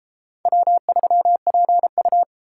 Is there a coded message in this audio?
Yes: Morse code from 0.5 to 2.2 s, heard as a keyed tone, about 31 wpm, 700 Hz. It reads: W3PU